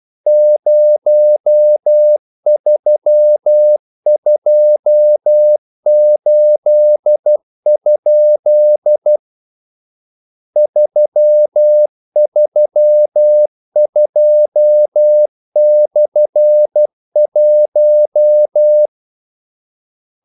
Morse code, heard as a keyed tone, about 12 wpm, 600 Hz. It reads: 0328? 332/1
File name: code-exemple